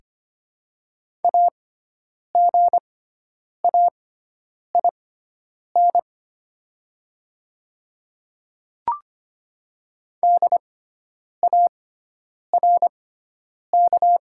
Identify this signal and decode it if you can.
Morse code, heard as a keyed tone, about 25 wpm, spaced as for 9 wpm, 700 Hz. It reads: AGAIN DARK
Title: 6x Character Spacing - Top 500 Words - 25wpm